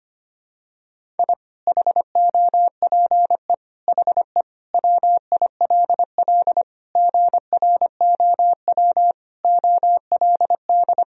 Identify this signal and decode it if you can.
Morse code, heard as a keyed tone, about 25 wpm, 700 Hz. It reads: I HOPE HE WILL GROW OLD
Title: Sentences from Top 1000 Words - Review Part 1 - 25wpm